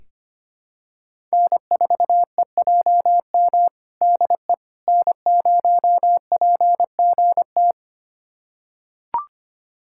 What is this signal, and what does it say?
Morse code, heard as a keyed tone, about 25 wpm, 700 Hz. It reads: N4EJM DE N0PGT
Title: USA Callsign DE Callsign - 5 Characters - 25wpm